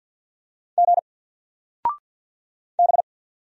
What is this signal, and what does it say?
Morse code, heard as a keyed tone, about 48 wpm, 700 Hz. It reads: G B
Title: Single Character (Letter:Number:Punctuation:Prosign) 50wpm